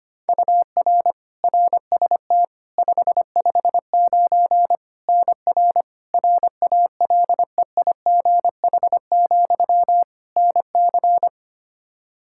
Morse code, heard as a keyed tone, about 25 wpm, 700 Hz. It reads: UR RST 559 NR RALEIGH, NC
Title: Practice QSO - Exchange 1 - Beginning - 25wpm